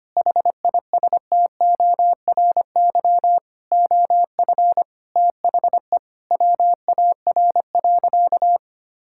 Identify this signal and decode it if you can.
Morse code, heard as a keyed tone, about 25 wpm, 700 Hz. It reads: HISTORY OF THE WAR.
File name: practice / 25wpm / Sentences with 4-Word Phrases - 25wpm